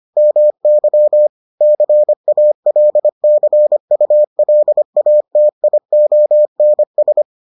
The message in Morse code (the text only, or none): MY CALCULATIONS